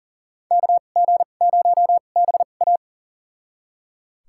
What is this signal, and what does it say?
Morse code, heard as a keyed tone, about 39 wpm, spaced as for 29 wpm, 700 Hz. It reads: KG0BA
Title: USA Call Signs - 5 Characters - 2x Character Spacing - 40wpm